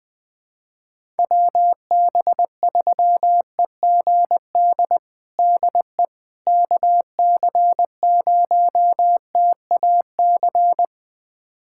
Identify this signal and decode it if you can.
Morse code, heard as a keyed tone, about 20 wpm, 700 Hz. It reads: WB3EGD DE KC0TAC
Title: USA Callsign DE Callsign - 6 Characters - 20wpm